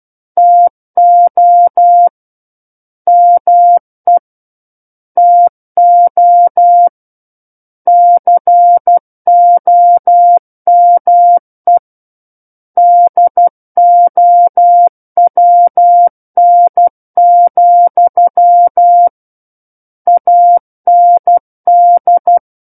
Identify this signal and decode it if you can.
Morse code, heard as a keyed tone, about 12 wpm, 700 Hz. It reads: TO ME TO COME DOWN, AND
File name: Book One / 03-Chapter 3 - 12 WPM